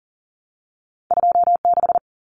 Morse code, heard as a keyed tone, about 40 wpm, 700 Hz. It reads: 26
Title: Common Cut Numbers - 2-Digit 40wpm